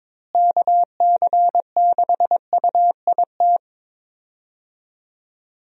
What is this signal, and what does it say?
Morse code, heard as a keyed tone, about 21 wpm, 700 Hz. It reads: KC6UIT
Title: USA Call Signs - 6 Characters - Repeated 2x - 22wpm